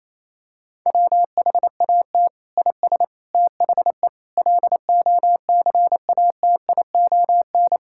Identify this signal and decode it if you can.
Morse code, heard as a keyed tone, about 28 wpm, 700 Hz. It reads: WHAT IS THE LOCATION
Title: Sentences from Top 1000 Words - Review Part 6 - 28wpm